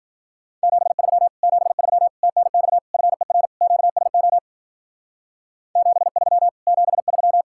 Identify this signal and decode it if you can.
Morse code, heard as a keyed tone, about 50 wpm, 700 Hz. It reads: 73 73 TNX FER QSO 73 73
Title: Speed-Racing Common QSO Phrases - 40wpm